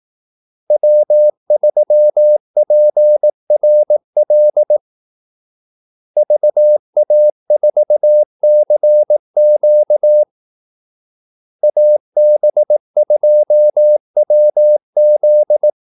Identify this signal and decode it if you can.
Morse code, heard as a keyed tone, about 18 wpm, 600 Hz. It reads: W3PRL VA4CQ AB2WZ
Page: Kaldesignaler 18 wpm